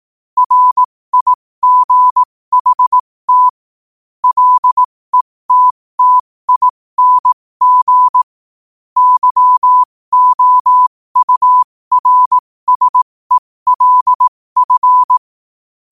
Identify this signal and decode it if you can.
Morse code, heard as a keyed tone, about 18 wpm, spaced as for 15 wpm, 1000 Hz. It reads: RIGHT LETTING YOURSELF
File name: qotdfiles / 15wpm